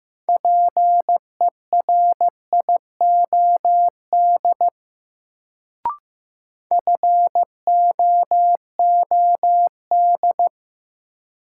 Morse code, heard as a keyed tone, about 15 wpm, 700 Hz. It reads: PERIOD FOOD
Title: Sets of 5 Words - New Word Drill - 15wpm